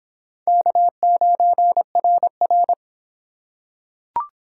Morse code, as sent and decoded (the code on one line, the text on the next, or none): -.- ----. .-. .-.
K9RR